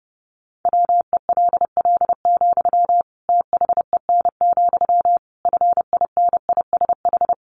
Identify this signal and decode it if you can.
Morse code, heard as a keyed tone, about 30 wpm, 700 Hz. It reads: WELL, THEN, FINISH
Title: Sentences from Top 1200 Words 30wpm